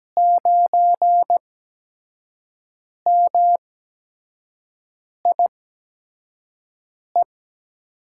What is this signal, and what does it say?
Morse code, heard as a keyed tone, about 17 wpm, spaced as for 5 wpm, 700 Hz. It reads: 9MIE
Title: USA Call Signs - 5 Characters - 8x Character Spacing - 17wpm